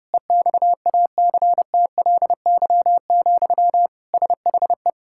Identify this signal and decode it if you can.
Morse code, heard as a keyed tone, about 30 wpm, 700 Hz. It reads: EXACTLY, SHE
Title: Sentences from Top 1000 Words - Review Part 6 - 30wpm